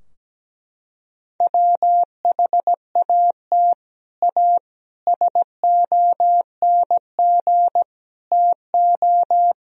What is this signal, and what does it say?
Morse code, heard as a keyed tone, about 17 wpm, 700 Hz. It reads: WHAT A SONG TO
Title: Sentences from Top 1000 Words - Review Part 2 - 17wpm